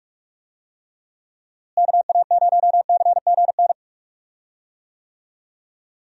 Morse code, heard as a keyed tone, about 46 wpm, 700 Hz. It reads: KA0CGN
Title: USA Call Signs - 6 Characters - Repeated 3x - 45wpm